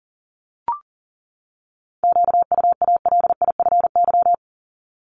Morse code, heard as a keyed tone, about 37 wpm, 700 Hz. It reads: QUALIFY
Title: Top 1751-2000 Words 40wpm